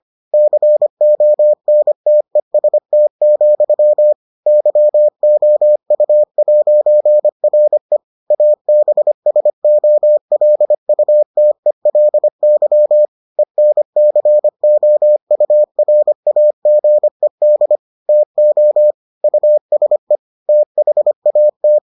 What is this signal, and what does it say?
Morse code, heard as a keyed tone, about 25 wpm, 600 Hz. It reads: CONTEST, YOU'RE ABSOLUTELY ENCOURAGED TO USE THAT